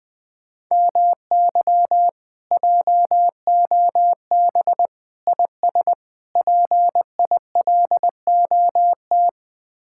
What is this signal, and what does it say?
Morse code, heard as a keyed tone, about 20 wpm, 700 Hz. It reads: MY JOB IS PILOT
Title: Speed-Racing Common QSO Phrases Encore - 20wpm